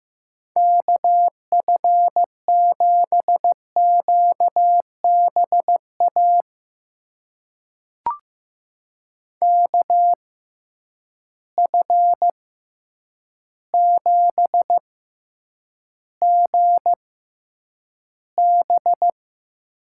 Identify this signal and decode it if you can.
Morse code, heard as a keyed tone, about 15 wpm, spaced as for 8 wpm, 700 Hz. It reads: KF7QBA KF7GB
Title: USA Call Signs - 6 Characters - 6x Character Spacing - 15wpm